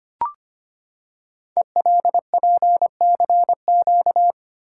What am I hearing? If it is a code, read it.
Morse code, heard as a keyed tone, about 25 wpm, 700 Hz. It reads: ELPCQ
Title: Code Groups - 5 Letters - 25wpm